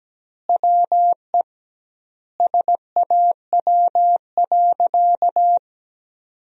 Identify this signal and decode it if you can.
Morse code, heard as a keyed tone, about 17 wpm, 700 Hz. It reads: WE SAW.
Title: 2x Word-Spacing - Sentences from Top 300 Words - 17wpm